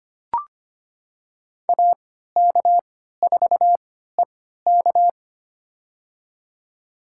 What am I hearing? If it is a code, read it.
Morse code, heard as a keyed tone, about 25 wpm, spaced as for 14 wpm, 700 Hz. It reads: AK4EK
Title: USA Call Signs - 3x Character Spacing - 25wpm